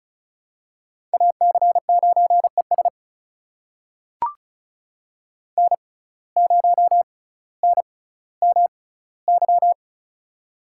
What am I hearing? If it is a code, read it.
Morse code, heard as a keyed tone, about 34 wpm, spaced as for 18 wpm, 700 Hz. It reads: AC9ES N0NMY
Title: USA Call Signs - 5 Characters - 6x Character Spacing - 35wpm